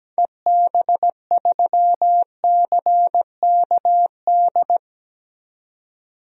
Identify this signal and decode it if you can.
Morse code, heard as a keyed tone, about 17 wpm, 700 Hz. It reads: EB3CKD
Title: DXCC-100 Call Signs - 6 Characters - 17wpm